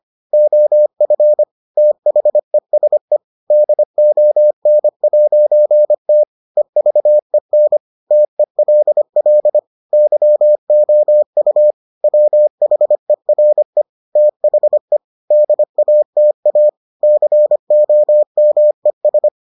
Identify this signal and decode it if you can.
Morse code, heard as a keyed tone, about 25 wpm, 600 Hz. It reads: OF THESE DON'T EVEN TELL YOU WHERE THE DATA COMES